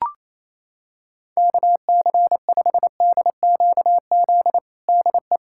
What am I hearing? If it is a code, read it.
Morse code, heard as a keyed tone, about 28 wpm, 700 Hz. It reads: KC5DQZ DE
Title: USA Callsign DE Callsign 28wpm